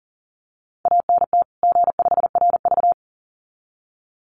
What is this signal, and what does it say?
Morse code, heard as a keyed tone, about 40 wpm, 700 Hz. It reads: ANT G5RV
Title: Common QSO Phrases 40wpm